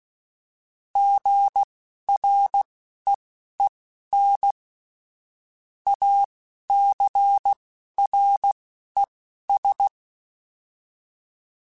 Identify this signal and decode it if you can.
Morse code, heard as a keyed tone, about 16 wpm, spaced as for 11 wpm, 800 Hz. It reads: GREEN ACRES